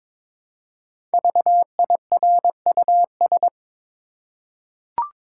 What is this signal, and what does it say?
Morse code, heard as a keyed tone, about 23 wpm, 700 Hz. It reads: VIRUS
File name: 5 Letter Words - First 3 Letters Given - 22wpm